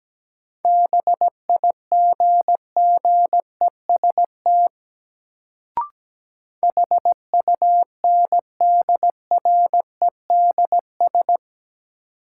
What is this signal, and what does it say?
Morse code, heard as a keyed tone, about 17 wpm, 700 Hz. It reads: BIGGEST HUNDREDS
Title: Sets of 4 Words - New Word Drill - 17wpm